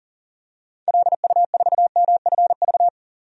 Morse code, heard as a keyed tone, about 40 wpm, 700 Hz. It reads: LU4MFV
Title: DXCC-100 Call Signs - 6 Characters - 40wpm